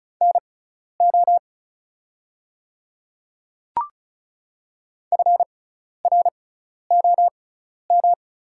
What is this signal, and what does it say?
Morse code, heard as a keyed tone, about 36 wpm, spaced as for 12 wpm, 700 Hz. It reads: NO FROM
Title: 6x Character Spacing - Top 100 Words - 35wpm